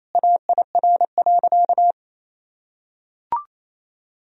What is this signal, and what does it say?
Morse code, heard as a keyed tone, about 28 wpm, 700 Hz. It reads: AIR.
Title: Sentences from Top 1000 Words - Review Part 1 - 28wpm